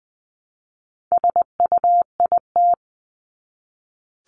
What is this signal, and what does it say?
Morse code, heard as a keyed tone, about 20 wpm, 700 Hz. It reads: SUIT
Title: Top 1501-1750 Words 20wpm